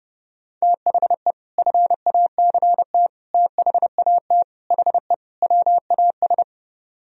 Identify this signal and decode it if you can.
Morse code, heard as a keyed tone, about 32 wpm, 700 Hz. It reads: THE FACT THAT HE WAS